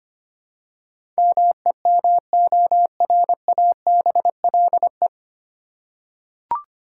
Morse code, heard as a keyed tone, about 25 wpm, 700 Hz. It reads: MEMORABLE
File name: practice / 25wpm / 9 Letter Words - First 3 Letters Given - 25wpm